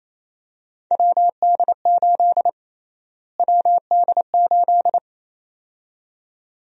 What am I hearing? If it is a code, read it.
Morse code, heard as a keyed tone, about 28 wpm, 700 Hz. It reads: WD8 WD8
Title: USA Call Sign Prefixes - Repeated 2x - 28wpm